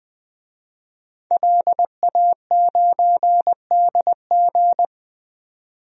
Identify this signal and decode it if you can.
Morse code, heard as a keyed tone, about 20 wpm, 700 Hz. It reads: LA9DG
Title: DXCC-100 Call Signs - 5 Characters - 20wpm